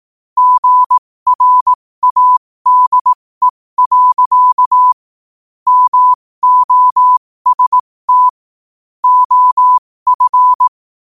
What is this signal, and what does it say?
Morse code, heard as a keyed tone, about 18 wpm, spaced as for 15 wpm, 1000 Hz. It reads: GRADE. MOST OF
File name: qotdfiles / 15wpm